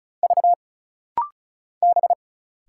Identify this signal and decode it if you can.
Morse code, heard as a keyed tone, about 36 wpm, 700 Hz. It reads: V B
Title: Single Letters - Rapid-Fire 35wpm